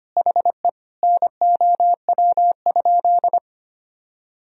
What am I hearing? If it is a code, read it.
Morse code, heard as a keyed tone, about 25 wpm, 700 Hz. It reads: HE NOW?
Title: Sentences from Top 100 Words and Abbreviations 25wpm